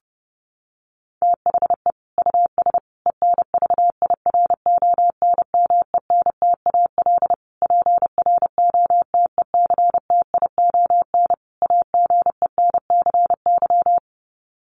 Morse code, heard as a keyed tone, about 30 wpm, 700 Hz. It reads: THE US ENVIRONMENTAL PROTECTION AGENCY
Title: Sets of 5 Words Encore2 30wpm